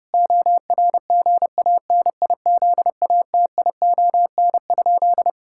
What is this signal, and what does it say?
Morse code, heard as a keyed tone, about 30 wpm, 700 Hz. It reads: ORGANIZATION?
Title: Sentences from Top 1000 Words - Review Part 5 - 30wpm